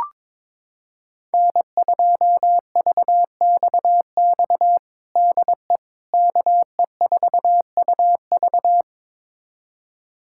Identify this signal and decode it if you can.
Morse code, heard as a keyed tone, about 22 wpm, 700 Hz. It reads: N2VXX DE KE4UV
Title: USA Callsign DE Callsign - 5 Characters - 22wpm